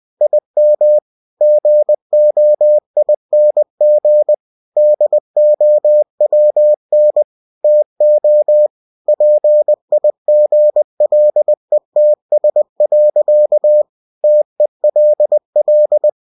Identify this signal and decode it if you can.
Morse code, heard as a keyed tone, about 20 wpm, 600 Hz. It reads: IM GOING DOWN TO PIGLETS. TELL
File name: cw-books / winnie-the-pooh / winnie-20wpm-page-0078